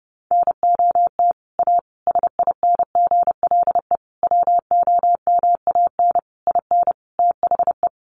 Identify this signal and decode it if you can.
Morse code, heard as a keyed tone, about 28 wpm, 700 Hz. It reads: NOT A SINGLE WOMAN IN THE
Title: Sentences from Top 1100 Words 30wpm